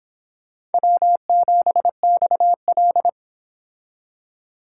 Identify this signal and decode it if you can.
Morse code, heard as a keyed tone, about 29 wpm, 700 Hz. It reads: W7XL